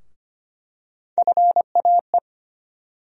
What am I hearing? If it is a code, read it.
Morse code, heard as a keyed tone, about 25 wpm, 700 Hz. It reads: FAE
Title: USA Call Sign Postfixes - 3 Letters 25wpm